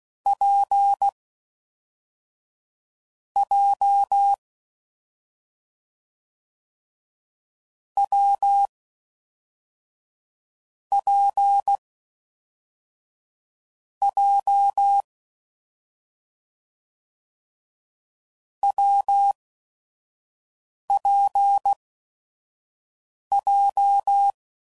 Morse code, heard as a keyed tone, about 16 wpm, spaced as for 5 wpm, 800 Hz. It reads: P J W P J WPJ
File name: Combinations / Difficult Combos / WPJ WPJ1